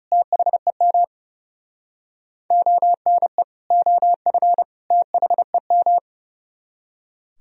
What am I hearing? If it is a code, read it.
Morse code, heard as a keyed tone, about 37 wpm, 700 Hz. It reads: THEM ONE OF THEM
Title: Speed-Racing - Sets of 3 Words 30wpm